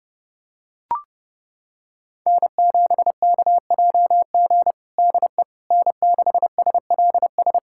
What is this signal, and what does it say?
Morse code, heard as a keyed tone, about 30 wpm, 700 Hz. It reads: N7KJG DE N6SLS
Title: USA Callsign DE Callsign - 5 Characters - 30wpm